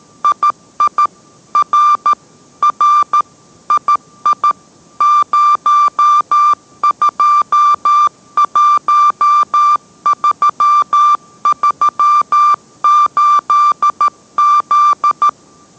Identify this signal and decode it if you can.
Morse code, heard as a keyed tone, about 13 wpm, 1180 Hz. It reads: II R R II 021338Z